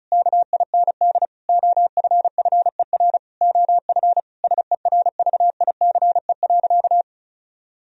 Morse code, heard as a keyed tone, about 34 wpm, 700 Hz. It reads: KIND OFFER OF SERVICE.
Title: Sentences from Top 1000 Words - Review Part 4 - 35wpm